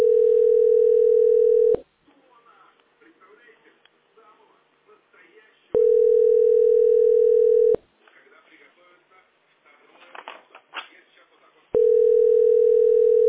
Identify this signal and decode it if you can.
Ringback tone (North America), 440 + 480 Hz, with speech.
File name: 590a0e32-5503-4ed7-9d00